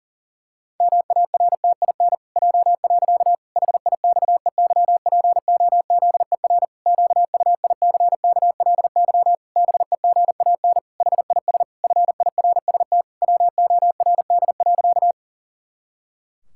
Morse code, heard as a keyed tone, about 37 wpm, 700 Hz. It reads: MARTIN J. HIXEYPOZER QUICKLY BEGAN HIS FIRST WORD.
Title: Pangram Challenge v2 - 40wpm